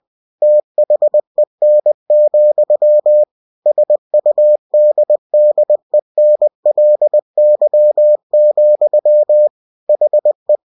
Morse code, heard as a keyed tone, about 20 wpm, 600 Hz. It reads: THEN, SUDDENLY, HE